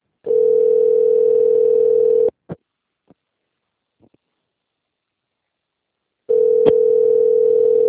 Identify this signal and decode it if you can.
Ringback tone (North America), 440 + 480 Hz.